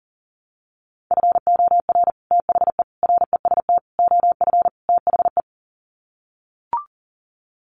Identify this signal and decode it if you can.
Morse code, heard as a keyed tone, about 37 wpm, 700 Hz. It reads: FOR THE REST OF THE
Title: Sets of 5 Words 40wpm